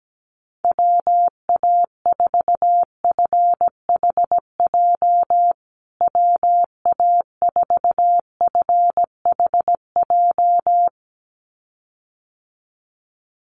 Morse code, heard as a keyed tone, about 17 wpm, 700 Hz. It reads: WA4FHJ WA4FHJ
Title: USA Call Signs Repeated 2x 17wpm